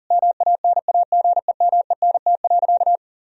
Morse code, heard as a keyed tone, about 40 wpm, 700 Hz. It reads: MANAGEMENT.
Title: Sentences from Top 1000 Words - Review Part 4 - 40wpm